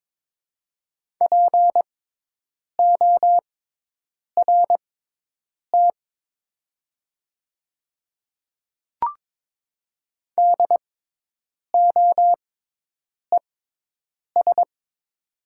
Morse code, heard as a keyed tone, about 22 wpm, spaced as for 8 wpm, 700 Hz. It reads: PORT DOES
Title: 6x Character Spacing - Top 200 Words - 22wpm